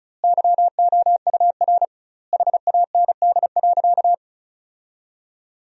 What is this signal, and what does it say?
Morse code, heard as a keyed tone, about 34 wpm, 700 Hz. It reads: YOUR HAND.
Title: 2x Word-Spacing - Sentences from Top 400 Words - 35wpm